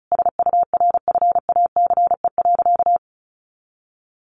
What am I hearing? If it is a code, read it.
Morse code, heard as a keyed tone, about 34 wpm, 700 Hz. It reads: SURFACE.
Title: Sentences from Top 600 Words 35wpm